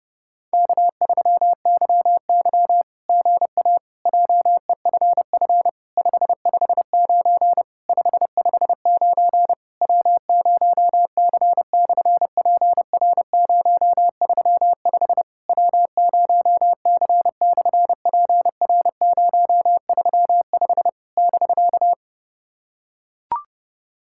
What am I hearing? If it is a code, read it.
Morse code, heard as a keyed tone, about 30 wpm, 700 Hz. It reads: K3YY GA JEFF 559 559 W0C/PR035 W0C/PR035 <BK>
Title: SOTA QSO Practice 30wpm